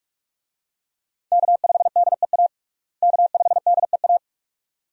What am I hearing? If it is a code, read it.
Morse code, heard as a keyed tone, about 47 wpm, 700 Hz. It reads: K5DEA K5DEA
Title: USA Call Signs - 5 Characters - Repeated 2x - 45wpm